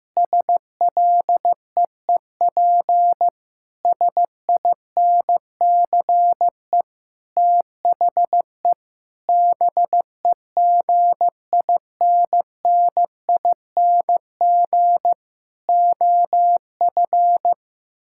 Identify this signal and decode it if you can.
Morse code, heard as a keyed tone, about 15 wpm, 700 Hz. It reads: SLEEP SINCE THE BEGINNING OF
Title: Sentences from Top 1000 Words - Review Part 6 - 15wpm